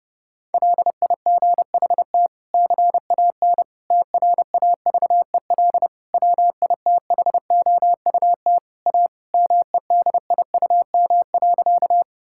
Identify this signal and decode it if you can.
Morse code, heard as a keyed tone, about 30 wpm, 700 Hz. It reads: LIGHT CAN TRAVEL WITHOUT A MEDIUM.
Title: Sentences from Top 1000 Words - Review Part 6 - 30wpm